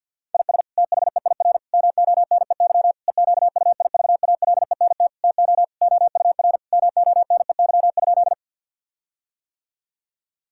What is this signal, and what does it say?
Morse code, heard as a keyed tone, about 50 wpm, 700 Hz. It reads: IS THEIR MONEY EQUIVALENT TO OUR MONEY?
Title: Sentences from Top 1600 Words 50wpm